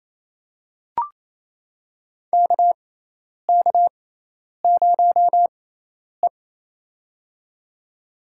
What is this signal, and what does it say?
Morse code, heard as a keyed tone, about 28 wpm, spaced as for 10 wpm, 700 Hz. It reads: KK0E
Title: USA Call Signs - 4 Characters - 6x Character Spacing - 28wpm